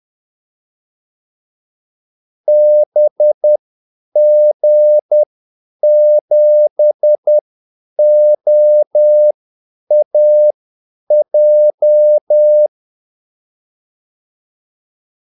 Morse code, heard as a keyed tone, about 10 wpm, spaced as for 8 wpm, 600 Hz. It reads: BG7OAJ
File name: Callsigns-10-8wpm